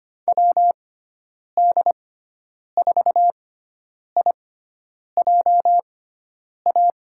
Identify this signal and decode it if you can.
Morse code, heard as a keyed tone, about 25 wpm, spaced as for 9 wpm, 700 Hz. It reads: WD4IJA
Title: USA Call Signs - 6 Characters - 6x Character Spacing - 25wpm